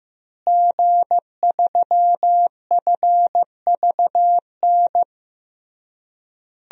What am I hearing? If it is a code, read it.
Morse code, heard as a keyed tone, about 16 wpm, 700 Hz. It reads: G3FVN
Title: DXCC-100 Call Signs - 5 Characters - Repeated 3x - 15wpm